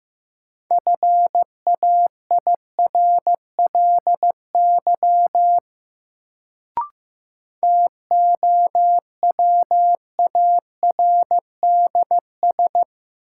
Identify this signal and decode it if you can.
Morse code, heard as a keyed tone, about 15 wpm, 700 Hz. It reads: FAIRLY TOWARDS
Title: Sentences from Top 1100 Words - New Word Drill Alt - 15wpm